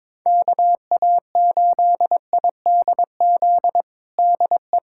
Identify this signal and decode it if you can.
Morse code, heard as a keyed tone, about 22 wpm, 700 Hz. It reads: KA8IDZ DE
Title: USA Callsign DE Callsign 22wpm